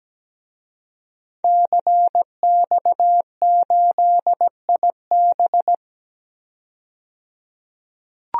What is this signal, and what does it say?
Morse code, heard as a keyed tone, about 17 wpm, 700 Hz. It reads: CX8IB
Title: DXCC-100 Call Signs - 5 Characters - 17wpm